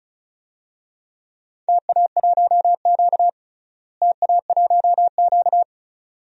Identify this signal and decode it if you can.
Morse code, heard as a keyed tone, about 34 wpm, 700 Hz. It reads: TA1Q TA1Q
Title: DXCC-100 Call Signs - 4 Characters - Repeated 2x - 35wpm